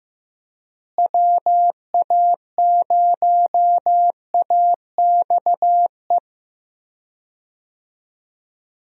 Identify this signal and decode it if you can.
Morse code, heard as a keyed tone, about 15 wpm, 700 Hz. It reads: WA0AXE
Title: USA Call Signs - 6 Characters - Repeated 2x - 15wpm